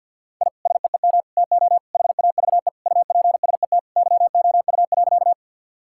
Morse code, heard as a keyed tone, about 50 wpm, 700 Hz. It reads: I SEEM TO HAVE UPSET YOU.
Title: Sentences from Top 1000 Words - Review Part 5 - 50wpm